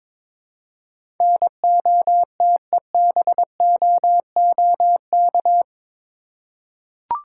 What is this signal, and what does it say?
Morse code, heard as a keyed tone, about 22 wpm, 700 Hz. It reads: NOTEBOOK
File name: practice / 22wpm / Eight Letter Words - 22wpm